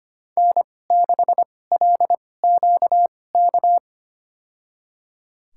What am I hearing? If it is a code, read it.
Morse code, heard as a keyed tone, about 25 wpm, spaced as for 18 wpm, 700 Hz. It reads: N6LQK
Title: USA Call Signs - 5 Characters - 2x Character Spacing - 25wpm